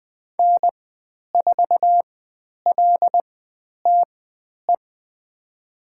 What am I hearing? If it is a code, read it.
Morse code, heard as a keyed tone, about 20 wpm, spaced as for 10 wpm, 700 Hz. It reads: N4LTE
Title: 102 Lesson 17g - Letter u - Call Signs 20wpm